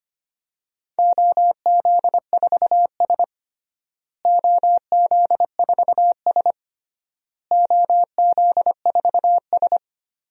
Morse code, heard as a keyed tone, about 25 wpm, 700 Hz. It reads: OZ4S OZ4S OZ4S
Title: DXCC-100 Call Signs - 4 Characters - Repeated 3x - 25wpm